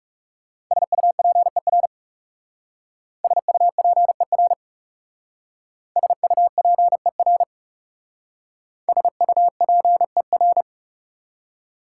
Morse code, heard as a keyed tone, about 37 wpm, 690 Hz. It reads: SUPER SUPER SUPER SUPER
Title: Speed-Racing Prefixes and Suffixes 30wpm